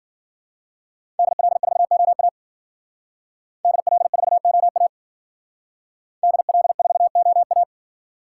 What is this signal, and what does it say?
Morse code, heard as a keyed tone, about 52 wpm, 700 Hz. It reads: DL4OA DL4OA DL4OA